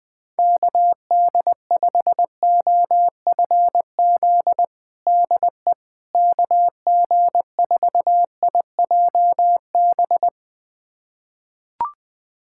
Morse code, heard as a keyed tone, about 20 wpm, 700 Hz. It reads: KD5OFZ DE KG4IJB
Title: USA Callsign DE Callsign - 6 Characters - 20wpm